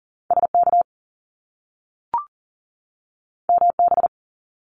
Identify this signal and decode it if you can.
Morse code, heard as a keyed tone, about 41 wpm, 700 Hz. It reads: SK MB
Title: Canadian Provinces and Territories 40wpm